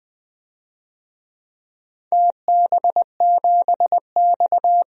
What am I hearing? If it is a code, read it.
Morse code, heard as a keyed tone, about 20 wpm, 700 Hz. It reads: TB7X
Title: DXCC-100 Call Signs - 4 Characters - 20wpm